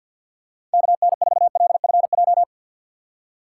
Morse code, heard as a keyed tone, about 52 wpm, 700 Hz. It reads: KN4LFJ
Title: USA Call Signs - 6 Characters - 6x Character Spacing - 50wpm